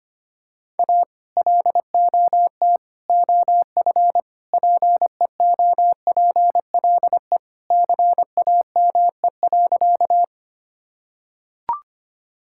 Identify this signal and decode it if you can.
Morse code, heard as a keyed tone, about 25 wpm, 700 Hz. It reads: A LOT OF PEOPLE CAME.
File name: practice / 25wpm / Sentences with 3-Word Phrases - 25wpm